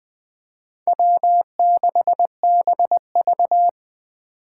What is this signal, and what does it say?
Morse code, heard as a keyed tone, about 20 wpm, 700 Hz. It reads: W6BV
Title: Four Challenging Letter-Number Combinations 20wpm